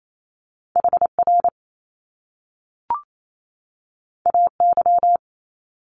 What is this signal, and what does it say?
Morse code, heard as a keyed tone, about 28 wpm, 700 Hz. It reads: HR AY
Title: Top Two Letter Combinations in Words 28wpm